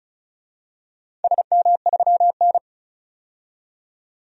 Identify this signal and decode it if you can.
Morse code, heard as a keyed tone, about 36 wpm, 700 Hz. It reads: SM3N
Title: DXCC-100 Call Signs - 4 Characters - 35wpm